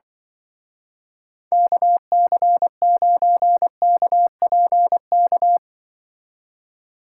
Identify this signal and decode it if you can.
Morse code, heard as a keyed tone, about 23 wpm, 700 Hz. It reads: KC9KPK
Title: Speed Racing - USA Call Signs 20wpm